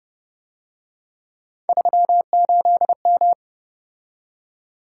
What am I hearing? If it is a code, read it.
Morse code, heard as a keyed tone, about 31 wpm, 700 Hz. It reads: 38M